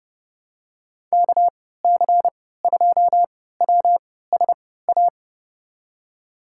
Morse code, heard as a keyed tone, about 32 wpm, spaced as for 17 wpm, 690 Hz. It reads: KC2WSA
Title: USA Call Signs - 3x Character Spacing - 30wpm